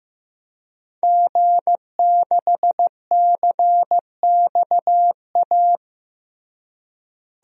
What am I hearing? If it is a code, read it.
Morse code, heard as a keyed tone, about 15 wpm, 700 Hz. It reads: G6CXA
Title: DXCC-100 Call Signs - 5 Characters - Repeated 2x - 15wpm